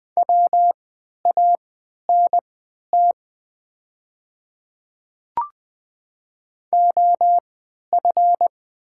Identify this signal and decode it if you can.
Morse code, heard as a keyed tone, about 20 wpm, spaced as for 11 wpm, 700 Hz. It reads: WANT OF
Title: 3x Character Spacing - Top 300 Words - 20wpm